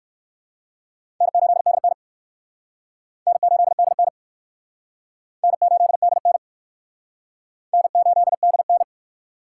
Morse code, heard as a keyed tone, about 56 wpm, 700 Hz. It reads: N8DN N8DN N8DN N8DN
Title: Speed Racing - USA Call Signs 45wpm